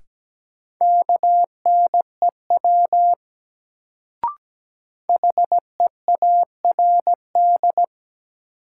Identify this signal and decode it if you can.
Morse code, heard as a keyed tone, about 17 wpm, 700 Hz. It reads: KNEW HEARD
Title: Sentences from Top 400 Words - New Word Drill - 17wpm